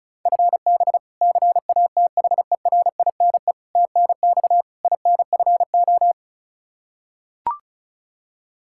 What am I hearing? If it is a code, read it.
Morse code, heard as a keyed tone, about 36 wpm, 700 Hz. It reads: FB CATHERINE TNX INFO
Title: K1USN SST QSO Practice 35wpm